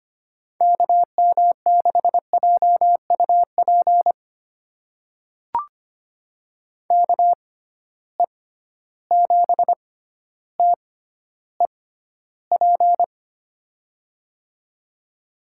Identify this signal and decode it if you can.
Morse code, heard as a keyed tone, about 25 wpm, spaced as for 13 wpm, 700 Hz. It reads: KM6JUP KE7TEP
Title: USA Call Signs - 6 Characters - 6x Character Spacing - 25wpm